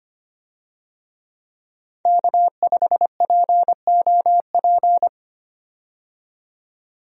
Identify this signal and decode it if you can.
Morse code, heard as a keyed tone, about 25 wpm, 700 Hz. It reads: K5POP
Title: USA Call Signs - 5 Characters - 25wpm